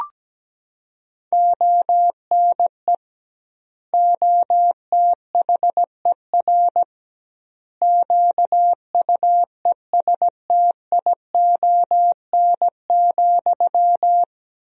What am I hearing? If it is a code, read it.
Morse code, heard as a keyed tone, about 17 wpm, 700 Hz. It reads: ONE OTHER QUESTION,
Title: 2x Word-Spacing - Sentences from Top 400 Words - 17wpm